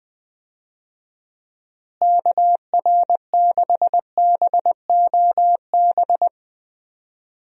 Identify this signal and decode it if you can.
Morse code, heard as a keyed tone, about 20 wpm, 700 Hz. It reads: KR6BOB
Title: USA Call Signs - 6 Characters - Repeated 3x - 20wpm